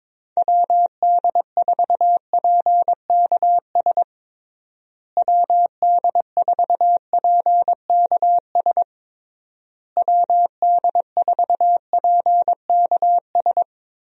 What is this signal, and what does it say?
Morse code, heard as a keyed tone, about 22 wpm, 700 Hz. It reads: WD4PKS WD4PKS WD4PKS
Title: USA Call Signs - 6 Characters - Repeated 3x - 22wpm